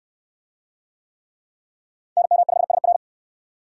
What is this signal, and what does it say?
Morse code, heard as a keyed tone, about 62 wpm, 700 Hz. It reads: NM5SR